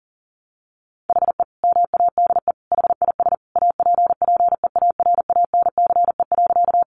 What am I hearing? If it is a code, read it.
Morse code, heard as a keyed tone, about 40 wpm, 700 Hz. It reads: HE MADE HIS APPEARANCE.
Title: Sentences from Top 1100 Words 40wpm